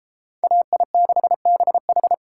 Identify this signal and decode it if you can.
Morse code, heard as a keyed tone, about 33 wpm, 700 Hz. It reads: AI6BH